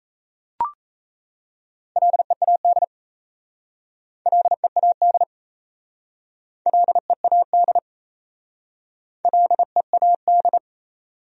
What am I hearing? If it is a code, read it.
Morse code, heard as a keyed tone, about 34 wpm, 700 Hz. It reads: LEAD LEAD LEAD LEAD
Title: Speed-Racing Top 500 Words 28wpm